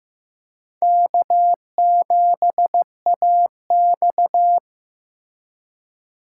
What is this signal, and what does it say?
Morse code, heard as a keyed tone, about 18 wpm, 700 Hz. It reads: K7AX
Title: Speed Racing - USA 1x2 Call Signs 15wpm